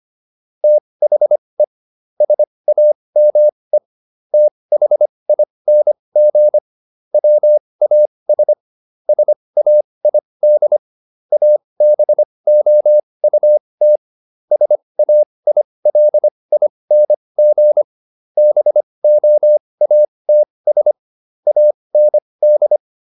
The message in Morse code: THE SAME THING WAS SAID ABOUT SAILING BOATS AND

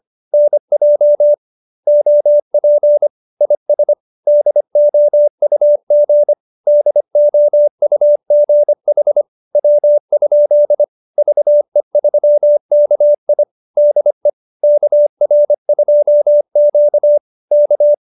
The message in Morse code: NJ OP IS DOUG DOUG HW? VE3KI DE KR2Q K

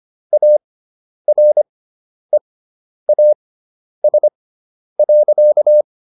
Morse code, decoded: AREAS.

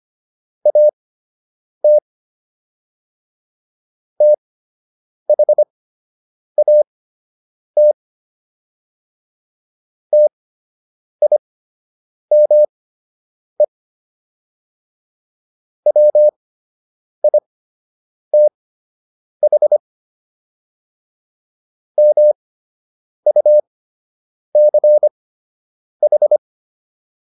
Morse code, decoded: AT THAT TIME WITH MUCH

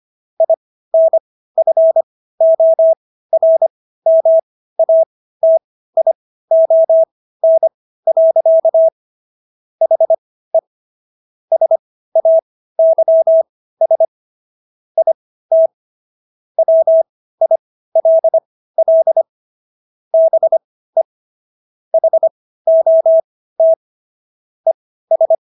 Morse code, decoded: INFORMATION. HE SAYS IT WILL BE HOT ES

